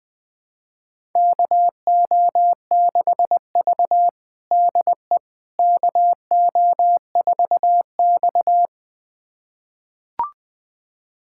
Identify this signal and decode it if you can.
Morse code, heard as a keyed tone, about 20 wpm, 700 Hz. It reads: KO6V DE KO4X
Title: USA Callsign DE Callsign - 4 Characters - 20wpm